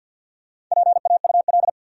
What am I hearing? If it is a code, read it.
Morse code, heard as a keyed tone, about 50 wpm, 700 Hz. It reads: PAUL
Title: USA Common Names - Max 5 Letters - 50wpm